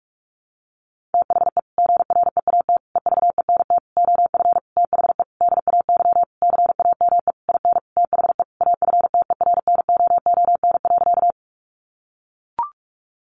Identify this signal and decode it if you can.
Morse code, heard as a keyed tone, about 44 wpm, 700 Hz. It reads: THE GREAT EVENT OF THE DAY CAME IN THE AFTERNOON.
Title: Sentences from Top 1500 Words 45wpm